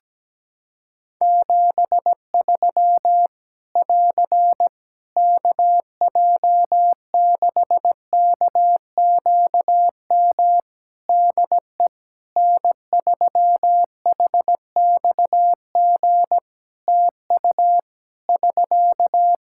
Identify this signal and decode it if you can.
Morse code, heard as a keyed tone, about 17 wpm, 700 Hz. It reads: 73 <AR> KJ6KQM DE N3HXG TU <SK>
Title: Practice QSO - Exchange 4 - Ending - 17wpm